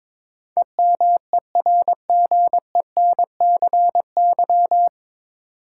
Morse code, heard as a keyed tone, about 22 wpm, 700 Hz. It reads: EMERGENCY
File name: practice / 22wpm / Nine Letter Words - 22wpm